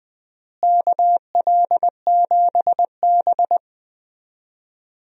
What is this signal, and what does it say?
Morse code, heard as a keyed tone, about 20 wpm, 700 Hz. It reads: KL7B
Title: DXCC-100 Call Signs - 4 Characters - Repeated 3x - 20wpm